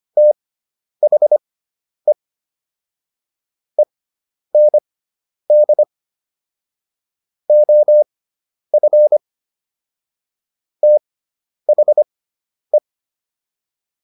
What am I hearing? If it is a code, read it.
Morse code, heard as a keyed tone, about 25 wpm, spaced as for 10 wpm, 600 Hz. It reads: THE END OF THE